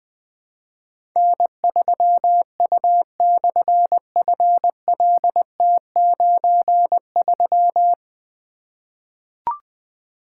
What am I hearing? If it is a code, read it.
Morse code, heard as a keyed tone, about 20 wpm, 700 Hz. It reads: N3U/FLT93
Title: Difficult Call Signs 20wpm